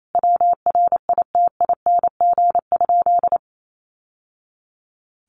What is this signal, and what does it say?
Morse code, heard as a keyed tone, about 28 wpm, 700 Hz. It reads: WRITING?
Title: Sentences from Top 1200 Words 28wpm